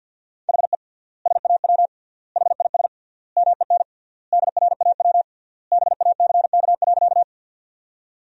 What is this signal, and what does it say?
Morse code, heard as a keyed tone, about 48 wpm, 700 Hz. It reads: HE SAW HIS MEN DRAW BACK.
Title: 3x Word-Spacing - Sentences from Top 300 Words - 50wpm